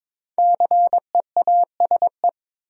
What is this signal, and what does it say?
Morse code, heard as a keyed tone, about 22 wpm, 700 Hz. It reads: CEASE
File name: 5 Letter Words - First 3 Letters Given - 22wpm